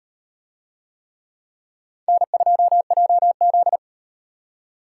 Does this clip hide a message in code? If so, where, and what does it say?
Yes: Morse code from 2.1 to 3.8 s, heard as a keyed tone, about 37 wpm, 700 Hz. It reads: N2JZ